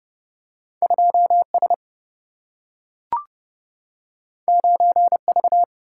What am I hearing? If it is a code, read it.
Morse code, heard as a keyed tone, about 30 wpm, 700 Hz. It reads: 2S 9V
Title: Single-Letter Single-Number 30wpm